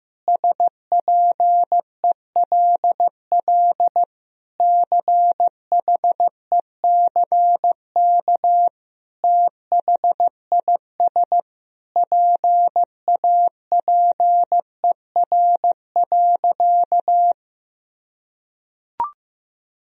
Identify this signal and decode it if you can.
Morse code, heard as a keyed tone, about 14 wpm, 700 Hz. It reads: SPELL CHECK THIS PAPER.
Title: Sentences from Top 1000 Words - Review Part 3 - 15wpm